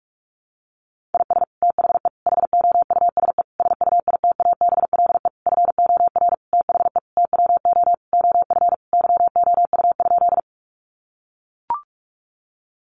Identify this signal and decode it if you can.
Morse code, heard as a keyed tone, about 44 wpm, 700 Hz. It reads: IS THE HOUSE SUITABLE FOR THE TWO OF YOU?
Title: Sentences from Top 1400 Words 45wpm